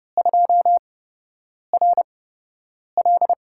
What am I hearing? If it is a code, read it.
Morse code, heard as a keyed tone, about 30 wpm, spaced as for 8 wpm, 700 Hz. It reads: 2RL
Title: USA Call Signs - 4 Characters - 8x Character Spacing - 30wpm